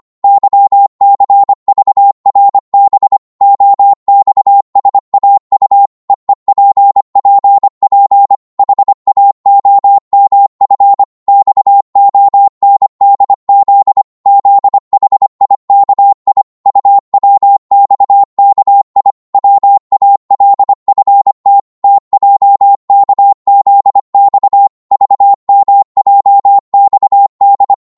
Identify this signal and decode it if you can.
Morse code, heard as a keyed tone, about 25 wpm, 800 Hz. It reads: YCVRB OXSAU EEPPP HAOMF XONDZ ZHIKI UWXKI WALFT TJKZX VMJXD
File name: Morse / ltr-25wpm